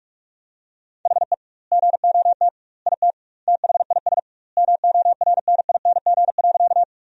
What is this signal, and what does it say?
Morse code, heard as a keyed tone, about 46 wpm, 700 Hz. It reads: HE GOT IT THIS MORNING.
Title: 2x Word-Spacing - Sentences from Top 400 Words - 45wpm